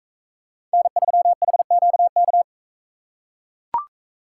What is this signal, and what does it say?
Morse code, heard as a keyed tone, about 44 wpm, 700 Hz. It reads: N3HQK